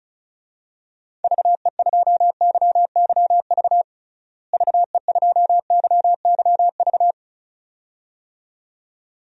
Morse code, heard as a keyed tone, about 36 wpm, 700 Hz. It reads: VE2YYV VE2YYV
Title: DXCC-100 Call Signs - 6 Characters - Repeated 2x - 35wpm